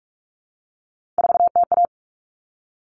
Morse code, heard as a keyed tone, about 44 wpm, 700 Hz. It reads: 4TA
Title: Common Cut Numbers - 3-Digit 45wpm